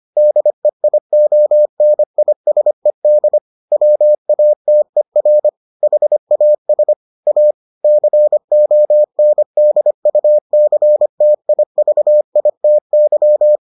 Morse code, heard as a keyed tone, about 25 wpm, 600 Hz. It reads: DEIONISED WATER HAS A CONDUCTIVITY